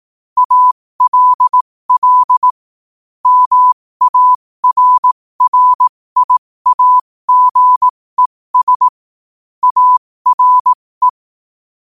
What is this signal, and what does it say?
Morse code, heard as a keyed tone, about 18 wpm, spaced as for 15 wpm, 1000 Hz. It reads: ALL MARRIAGES ARE
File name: qotdfiles / 15wpm